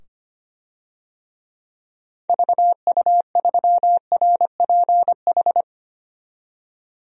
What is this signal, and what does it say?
Morse code, heard as a keyed tone, about 25 wpm, 700 Hz. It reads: VU3RPH
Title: DXCC-100 Call Signs - 6 Characters - 25wpm